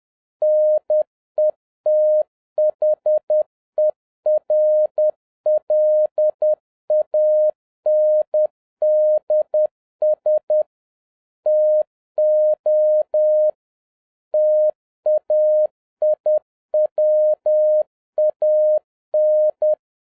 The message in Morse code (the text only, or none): NETHERLANDS TO TAIWAN